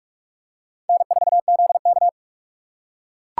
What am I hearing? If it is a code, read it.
Morse code, heard as a keyed tone, about 44 wpm, 700 Hz. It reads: N4ZK